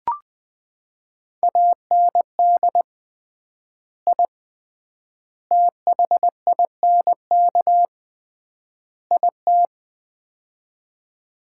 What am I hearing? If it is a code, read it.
Morse code, heard as a keyed tone, about 20 wpm, 700 Hz. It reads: AND I THINK IT
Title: 3x Word-Spacing - Sets of 4 Words - 20wpm